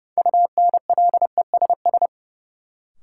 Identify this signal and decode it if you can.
Morse code, heard as a keyed tone, about 30 wpm, 700 Hz. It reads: UNLESS
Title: Sentences from Top 1000 Words - Review Part 5 - New Word Drill - 30wpm